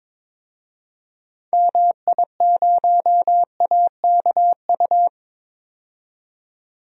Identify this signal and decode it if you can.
Morse code, heard as a keyed tone, about 21 wpm, 700 Hz. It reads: MI0AKU
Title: DXCC-100 Call Signs - 6 Characters - 22wpm